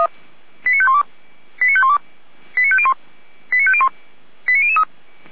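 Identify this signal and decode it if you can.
Five-tone selective call (ZVEI), heard as quick runs of five tones, tones of about 70 ms. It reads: 89512 89512 89501 89501 89003
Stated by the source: Five-tone ZVEI selective call received by SDR, from Feuerwehr Orlamünde audio archive.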